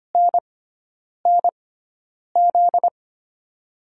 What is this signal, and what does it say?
Morse code, heard as a keyed tone, about 25 wpm, spaced as for 9 wpm, 700 Hz. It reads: NNZ
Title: USA Call Signs - 6x Character Spacing - 25wpm